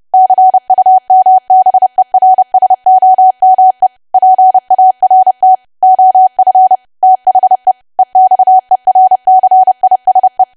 Morse code, heard as a keyed tone, about 30 wpm, 750 Hz. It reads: CUMBERSOME PART OF THE EXERCISE
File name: cwa / W1AW-Level3 / 30WPM